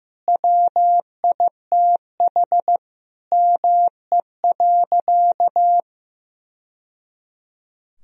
Morse code, heard as a keyed tone, about 15 wpm, 700 Hz. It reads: WITH ME.
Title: Sentences from Top 1000 Words - Review Part 2 - 15wpm